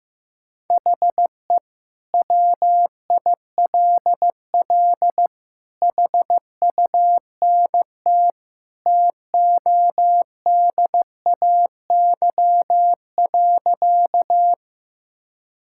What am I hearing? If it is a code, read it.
Morse code, heard as a keyed tone, about 15 wpm, 700 Hz. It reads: HE WILL HUNT TODAY.
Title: Sentences from Top 1600 Words 15wpm